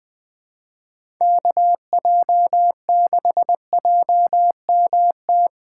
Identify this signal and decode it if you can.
Morse code, heard as a keyed tone, about 20 wpm, 700 Hz. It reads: KJ6JMT
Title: USA Call Signs - 6 Characters - 3x Character Spacing - 20wpm